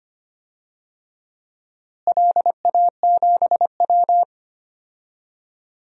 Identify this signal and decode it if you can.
Morse code, heard as a keyed tone, about 25 wpm, 700 Hz. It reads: LA7W
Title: DXCC-100 Call Signs - 4 Characters - Repeated 3x - 25wpm